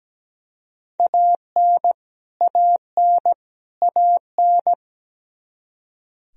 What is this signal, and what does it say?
Morse code, heard as a keyed tone, about 17 wpm, 700 Hz. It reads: AN AN AN
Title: Top 100 Words Repeated 17wpm